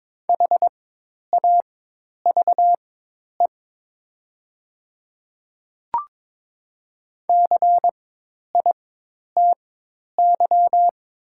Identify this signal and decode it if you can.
Morse code, heard as a keyed tone, about 22 wpm, spaced as for 10 wpm, 700 Hz. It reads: HAVE CITY
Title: 4x Character Spacing - Top 300 Words - 22wpm